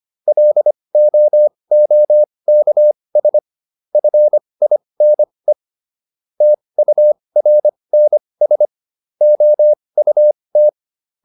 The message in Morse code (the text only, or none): LOOKS FINE TURNS OUT